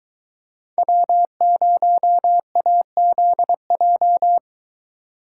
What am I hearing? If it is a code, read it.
Morse code, heard as a keyed tone, about 21 wpm, 700 Hz. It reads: W0AZJ